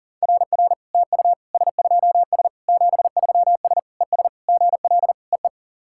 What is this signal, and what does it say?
Morse code, heard as a keyed tone, about 40 wpm, 690 Hz. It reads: RR TU S2S 73S ES GL EE
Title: SOTA QSO Practice 40wpm